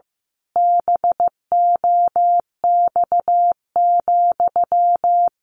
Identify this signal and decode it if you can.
Morse code, heard as a keyed tone, about 15 wpm, 700 Hz. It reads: BOX,
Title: A Sound Of Thunder 15wpm